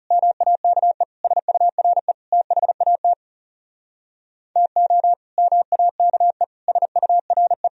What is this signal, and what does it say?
Morse code, heard as a keyed tone, about 40 wpm, 700 Hz. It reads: MAKE SURE THAT TO MAKE SURE